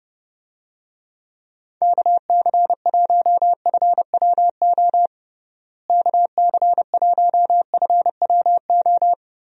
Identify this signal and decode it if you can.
Morse code, heard as a keyed tone, about 30 wpm, 700 Hz. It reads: KC1FWO KC1FWO
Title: USA Call Signs - 6 Characters - Repeated 3x - 30wpm